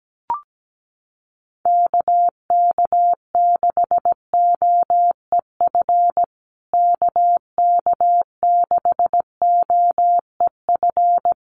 Morse code, heard as a keyed tone, about 17 wpm, 700 Hz. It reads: KK6OEF KK6OEF
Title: USA Call Signs Repeated 3x 17wpm